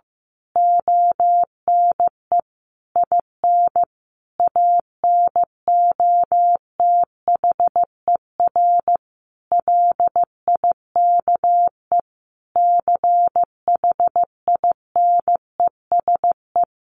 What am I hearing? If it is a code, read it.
Morse code, heard as a keyed tone, about 15 wpm, 700 Hz. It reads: ONE IN ANOTHER LIKE CHINESE
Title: A Sound Of Thunder 15wpm